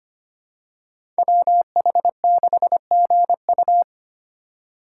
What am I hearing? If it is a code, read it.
Morse code, heard as a keyed tone, about 25 wpm, 700 Hz. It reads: WH6GU
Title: USA Call Signs - 5 Characters - 2x Character Spacing - 25wpm